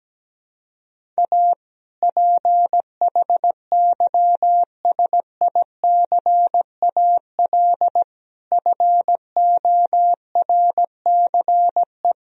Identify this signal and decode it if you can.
Morse code, heard as a keyed tone, about 17 wpm, 700 Hz. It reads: A PHYSICAL FORCE
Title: Sentences from Top 1000 Words - Review Part 4 - 17wpm